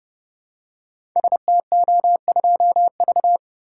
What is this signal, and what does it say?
Morse code, heard as a keyed tone, about 29 wpm, 700 Hz. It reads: STO2V
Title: Code Groups - 5 Letter:Numbers - 30wpm